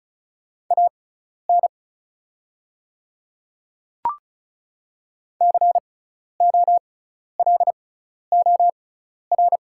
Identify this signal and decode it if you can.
Morse code, heard as a keyed tone, about 34 wpm, spaced as for 12 wpm, 690 Hz. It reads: AN COLOR
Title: 6x Character Spacing - Top 300 Words - 35wpm